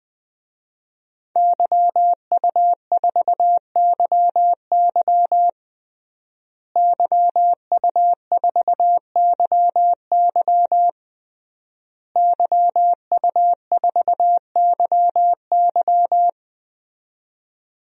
Morse code, heard as a keyed tone, about 20 wpm, 700 Hz. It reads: YU4YY YU4YY YU4YY
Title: DXCC-100 Call Signs - 5 Characters - Repeated 3x - 20wpm